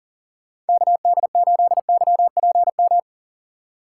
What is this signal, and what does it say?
Morse code, heard as a keyed tone, about 48 wpm, 700 Hz. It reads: KD8YPM
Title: Speed Racing - USA 2x3 Call Signs 40wpm